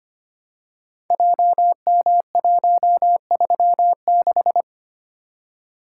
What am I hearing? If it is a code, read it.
Morse code, heard as a keyed tone, about 25 wpm, 700 Hz. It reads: JM136
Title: Code Groups - 5 Letter:Numbers - 25wpm